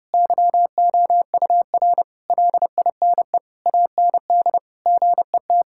Morse code, heard as a keyed tone, about 30 wpm, 700 Hz. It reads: YOUR LINE AND GET
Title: Sentences from Top 1000 Words - Review Part 6 - 30wpm